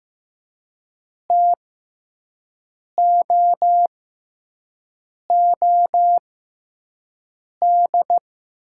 Morse code, heard as a keyed tone, about 15 wpm, spaced as for 5 wpm, 700 Hz. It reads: TOOD
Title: 6x Character Spacing - Top 500 Words - 15wpm